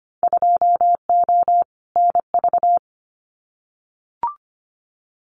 Morse code, heard as a keyed tone, about 25 wpm, 700 Hz. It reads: 2O NV
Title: Winter Field Day Exchanges 25wpm